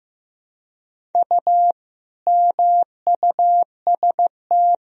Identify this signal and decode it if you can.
Morse code, heard as a keyed tone, about 15 wpm, 700 Hz. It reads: U MUST
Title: Sentences from Top 400 Words and Abbreviations 15wpm